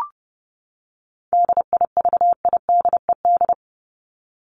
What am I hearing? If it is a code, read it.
Morse code, heard as a keyed tone, about 30 wpm, 700 Hz. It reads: DIVIDED
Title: ED Suffix - Top 100 Words with Base Words 30wpm